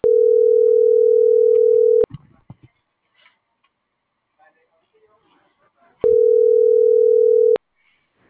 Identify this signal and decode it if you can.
Ringback tone (North America), 440 + 480 Hz.